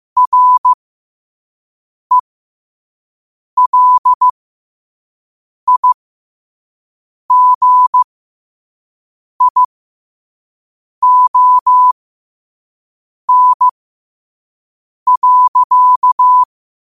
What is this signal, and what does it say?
Morse code, heard as a keyed tone, about 15 wpm, spaced as for 5 wpm, 1000 Hz. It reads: RELIGION.
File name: qotdfiles / 5wpm